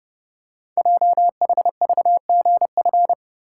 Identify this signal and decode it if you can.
Morse code, heard as a keyed tone, about 29 wpm, 690 Hz. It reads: JHVGF
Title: Code Groups - 5 Letters - 30wpm